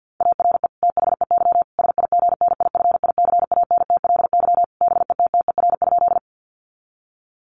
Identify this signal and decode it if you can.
Morse code, heard as a keyed tone, about 50 wpm, 700 Hz. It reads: ARE THEY SIGNIFICANTLY BETTER?
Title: Sentences from Top 1500 Words 50wpm